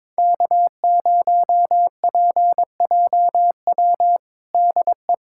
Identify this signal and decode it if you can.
Morse code, heard as a keyed tone, about 22 wpm, 700 Hz. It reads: K0PJW DE
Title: USA Callsign DE Callsign - 5 Characters - 22wpm